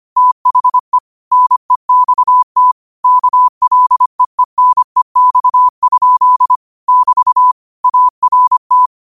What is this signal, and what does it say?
Morse code, heard as a keyed tone, about 25 wpm, 1000 Hz. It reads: THE NEXT KLEENEX? <BT> ART